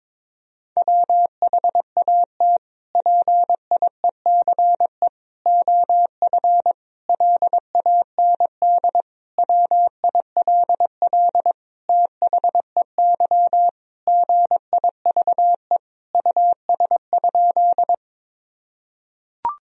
Morse code, heard as a keyed tone, about 22 wpm, 700 Hz. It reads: WHAT PIECE OF LAND WILL THEY GIVE US?
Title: Sentences from Top 1000 Words - Review Part 2 - 22wpm